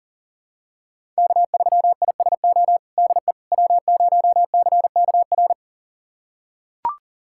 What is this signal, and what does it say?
Morse code, heard as a keyed tone, about 40 wpm, 700 Hz. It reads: K3ISO DE W0CKR
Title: USA Callsign DE Callsign - 5 Characters - 40wpm